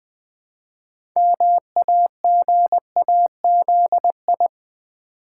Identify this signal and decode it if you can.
Morse code, heard as a keyed tone, about 20 wpm, 700 Hz. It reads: MAGAZI
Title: Build A Word - 2 Letters At A Time - 8-Letter Words - 20wpm